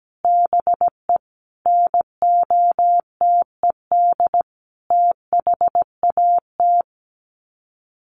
Morse code, heard as a keyed tone, about 17 wpm, 700 Hz. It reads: BE NOTED THAT
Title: Sets of 5 Words 17wpm